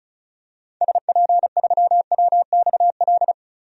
Morse code, heard as a keyed tone, about 34 wpm, 700 Hz. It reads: SP3WXL
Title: DXCC-100 Call Signs - 6 Characters - 35wpm